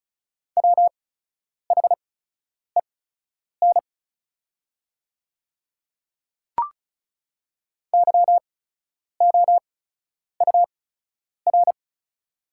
Morse code, heard as a keyed tone, about 36 wpm, spaced as for 10 wpm, 700 Hz. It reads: WHEN YOUR
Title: 8x Character Spacing - Top 100 Words - 35wpm